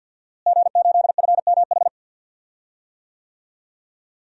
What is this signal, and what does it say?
Morse code, heard as a keyed tone, about 47 wpm, 700 Hz. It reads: G8FMH
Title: DXCC-100 Call Signs - 5 Characters - Repeated 3x - 50wpm